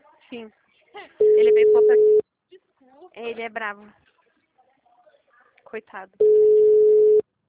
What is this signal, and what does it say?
Ringback tone (Europe, 425 Hz), with speech.
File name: out- -2021- - - .1346